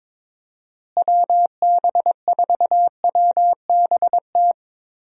Morse code, heard as a keyed tone, about 22 wpm, 700 Hz. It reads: WB4WBT
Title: USA Call Signs - 6 Characters - Repeated 3x - 22wpm